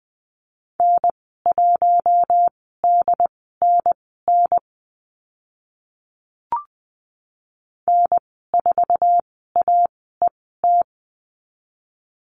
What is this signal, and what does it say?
Morse code, heard as a keyed tone, about 20 wpm, spaced as for 14 wpm, 700 Hz. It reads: N1DNN N4AET
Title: 076 Lesson 13g - Letter L - Call Signs 20wpm